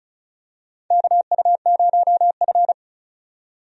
Morse code, heard as a keyed tone, about 36 wpm, 700 Hz. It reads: KU0F
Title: USA Call Signs - 4 Characters - 35wpm